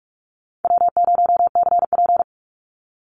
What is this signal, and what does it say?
Morse code, heard as a keyed tone, about 44 wpm, 700 Hz. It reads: W0CP
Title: NA SOTA Activator Callsigns 45wpm